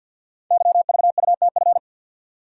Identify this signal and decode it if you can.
Morse code, heard as a keyed tone, about 52 wpm, 700 Hz. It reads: YVUTF
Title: Code Groups - 5 Letter:Numbers - 50wpm